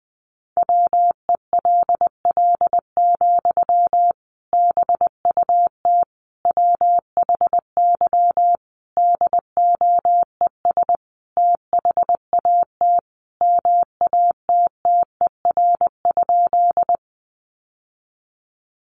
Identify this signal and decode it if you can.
Morse code, heard as a keyed tone, about 19 wpm, 700 Hz. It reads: WELL, BUT WHY DOES THAT MATTER?
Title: Sentences from Top 700 Words 20wpm